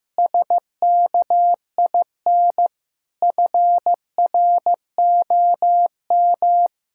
Morse code, heard as a keyed tone, about 16 wpm, 700 Hz. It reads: SKIN FROM
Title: Sentences from Top 1600 Words 15wpm